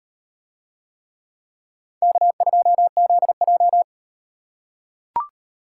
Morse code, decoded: K2ZJ

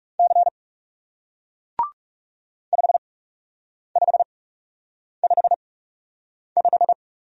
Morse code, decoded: C 5555

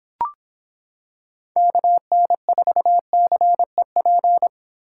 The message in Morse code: KN4CEP

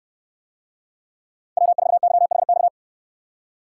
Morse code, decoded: W4YHX

